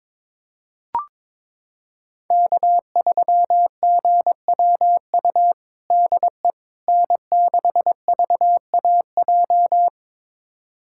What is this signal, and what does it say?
Morse code, heard as a keyed tone, about 22 wpm, 700 Hz. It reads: K3GWU DE N6VAJ
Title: USA Callsign DE Callsign - 5 Characters - 22wpm